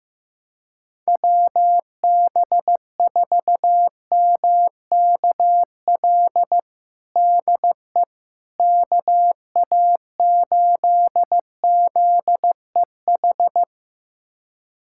Morse code, heard as a keyed tone, about 15 wpm, 700 Hz. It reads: WB4MKL DE KA8ZEH
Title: USA Callsign DE Callsign - 6 Characters - 15wpm